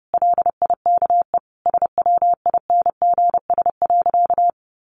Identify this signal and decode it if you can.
Morse code, heard as a keyed tone, about 30 wpm, 700 Hz. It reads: LIKE SWINGS.
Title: Easy Phrases 30wpm